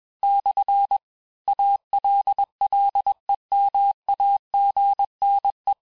Morse code, heard as a keyed tone, about 21 wpm, 800 Hz. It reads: / ALLEMAGNE
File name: texte-clair-vitesse-15mpm-puis-20mpm